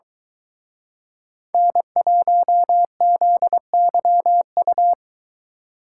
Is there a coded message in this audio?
Yes: Morse code from 1.5 to 4.9 s, heard as a keyed tone, about 23 wpm, 700 Hz. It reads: N1ZYU